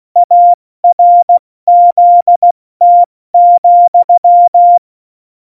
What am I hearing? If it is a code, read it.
Morse code, heard as a keyed tone, about 16 wpm, 700 Hz. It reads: ARZT,
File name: Morsekurs / Witze / Witz 27. Oktober 2020, 70 F80